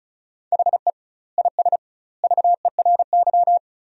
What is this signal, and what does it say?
Morse code, heard as a keyed tone, about 34 wpm, 700 Hz. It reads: HE IS VERY
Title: 2x Word-Spacing - Sentences from Top 400 Words - 35wpm